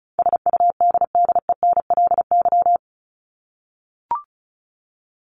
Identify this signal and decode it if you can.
Morse code, heard as a keyed tone, about 34 wpm, 700 Hz. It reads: SUDDENLY
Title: Top 1501-1750 Words 35wpm